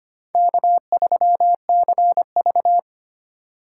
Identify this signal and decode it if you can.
Morse code, heard as a keyed tone, about 25 wpm, 700 Hz. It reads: K3CV
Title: USA Call Signs - 4 Characters - 3x Character Spacing - 25wpm